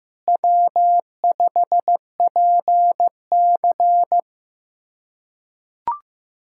Morse code, heard as a keyed tone, about 16 wpm, 700 Hz. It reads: W5PC
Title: USA Callsign DE Callsign - 4 Characters - 15wpm